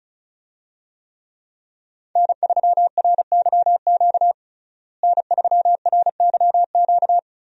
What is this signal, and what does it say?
Morse code, heard as a keyed tone, about 36 wpm, 700 Hz. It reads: N3RYQ N3RYQ
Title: USA Call Signs - 5 Characters - Repeated 2x - 35wpm